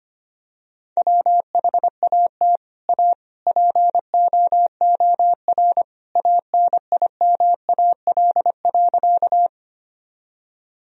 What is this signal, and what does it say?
Morse code, heard as a keyed tone, about 25 wpm, 700 Hz. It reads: WHAT A POOR ANIMAL.
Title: Sentences from Top 1000 Words - Review Part 1 - 25wpm